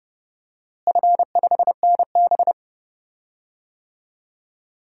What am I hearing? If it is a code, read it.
Morse code, heard as a keyed tone, about 32 wpm, 700 Hz. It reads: F5NB
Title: DXCC-100 Call Signs - 4 Characters - Repeated 2x - 30wpm